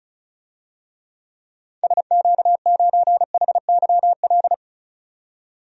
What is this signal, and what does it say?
Morse code, heard as a keyed tone, about 34 wpm, 700 Hz. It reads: SQ9HYL
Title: DXCC-100 Call Signs - 6 Characters - 35wpm